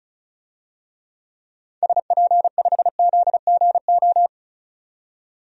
Morse code, heard as a keyed tone, about 34 wpm, 700 Hz. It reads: SP5ZGO
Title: DXCC-100 Call Signs - 6 Characters - 35wpm